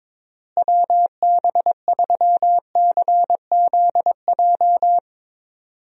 Morse code, heard as a keyed tone, about 22 wpm, 700 Hz. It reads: WB3CZJ
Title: USA Call Signs - 6 Characters - Repeated 2x - 22wpm